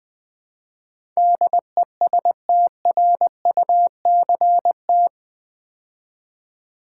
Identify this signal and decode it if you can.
Morse code, heard as a keyed tone, about 20 wpm, 700 Hz. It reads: DESTRUCT
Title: Build A Word - 4 Letters At A Time - 11-Letter Words - 20wpm